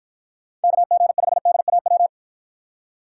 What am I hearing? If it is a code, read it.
Morse code, heard as a keyed tone, about 54 wpm, 700 Hz. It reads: KG5DAW